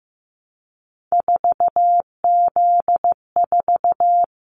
Morse code, heard as a keyed tone, about 15 wpm, 700 Hz. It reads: 4Z4
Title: Callsign Prefixes for Top 50 DXCC - 15wpm